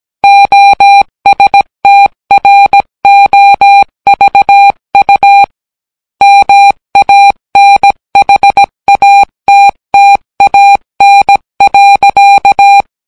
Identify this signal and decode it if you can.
Morse code, heard as a keyed tone, about 17 wpm, 800 Hz. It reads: OSTROVU MANHATTAN.